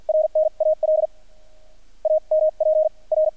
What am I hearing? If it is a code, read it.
Morse code, heard as a keyed tone, about 50 wpm, spaced as for 42 wpm, 620 Hz. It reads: RNAL AMP R
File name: archived / CRSnet